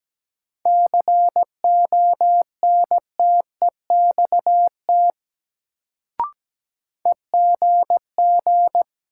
Morse code, heard as a keyed tone, about 17 wpm, 700 Hz. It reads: CONTEXT EGG
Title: Sentences from Top 1000 Words - Review Part 7 - New Word Drill - 17wpm